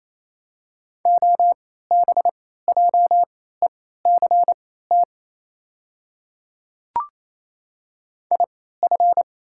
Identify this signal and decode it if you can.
Morse code, heard as a keyed tone, about 28 wpm, spaced as for 16 wpm, 690 Hz. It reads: OBJECT IF
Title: 3x Character Spacing - Top 500 Words - 28wpm